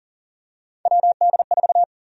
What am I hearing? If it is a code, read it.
Morse code, heard as a keyed tone, about 37 wpm, 700 Hz. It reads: WD4
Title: USA Call Sign Prefixes - Repeated 2x - 40wpm